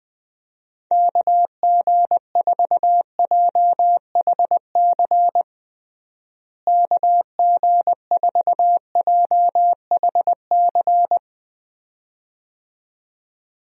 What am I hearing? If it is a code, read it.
Morse code, heard as a keyed tone, about 20 wpm, 700 Hz. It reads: KG4JHC KG4JHC
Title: USA Call Signs - 6 Characters - Repeated 2x - 20wpm